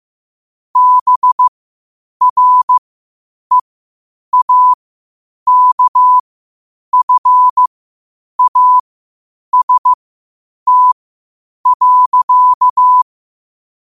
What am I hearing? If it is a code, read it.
Morse code, heard as a keyed tone, about 15 wpm, spaced as for 8 wpm, 1000 Hz. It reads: BREAKFAST.
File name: qotdfiles / 8wpm